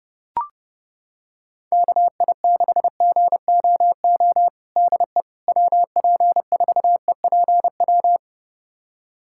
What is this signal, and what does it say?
Morse code, heard as a keyed tone, about 30 wpm, 700 Hz. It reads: KI6GOO DE WP4EPW
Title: USA Callsign DE Callsign - 6 Characters - 30wpm